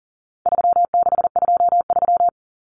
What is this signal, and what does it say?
Morse code, heard as a keyed tone, about 40 wpm, 700 Hz. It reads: 3623
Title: Numbers - 4-Digit 40wpm